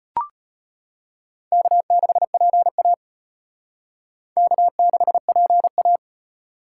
Morse code, heard as a keyed tone, about 31 wpm, 700 Hz. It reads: K6PA K6PA
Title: Speed Racing - USA Call Signs 25wpm